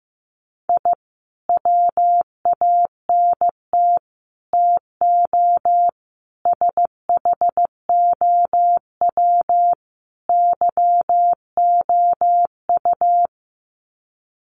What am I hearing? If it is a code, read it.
Morse code, heard as a keyed tone, about 15 wpm, 700 Hz. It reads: I WANT TO SHOW YOU
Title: Sets of 5 Words 15wpm